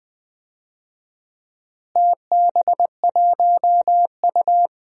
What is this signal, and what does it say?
Morse code, heard as a keyed tone, about 20 wpm, 700 Hz. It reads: TB1U
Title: DXCC-100 Call Signs - 4 Characters - Repeated 2x - 20wpm